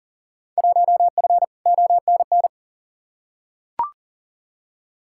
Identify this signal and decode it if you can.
Morse code, heard as a keyed tone, about 38 wpm, 700 Hz. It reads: 1F ONN
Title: Field Day Exchanges Encore - 40wpm